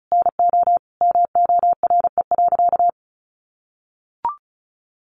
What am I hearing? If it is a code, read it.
Morse code, heard as a keyed tone, about 34 wpm, 700 Hz. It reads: NO MORE.
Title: Sentences from Top 1100 Words 35wpm